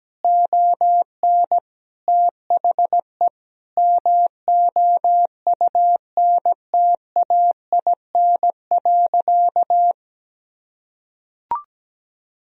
Morse code, heard as a keyed tone, about 17 wpm, 700 Hz. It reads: ON THE MOUNTAIN.
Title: Sentences from Top 1000 Words - Review Part 1 - 17wpm